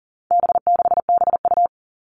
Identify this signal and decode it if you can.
Morse code, heard as a keyed tone, about 40 wpm, 700 Hz. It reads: B6BU
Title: All Cut Numbers - 4-Digit 40wpm